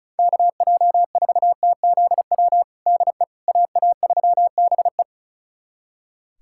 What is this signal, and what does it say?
Morse code, heard as a keyed tone, about 34 wpm, 700 Hz. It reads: KJ4TZW DE AA3BE
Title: USA Callsign DE Callsign 35wpm